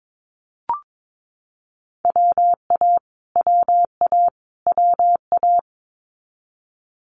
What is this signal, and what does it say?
Morse code, heard as a keyed tone, about 22 wpm, 700 Hz. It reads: WA WA WA
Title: Multi-letter Phonemes Repeated 3x 22wpm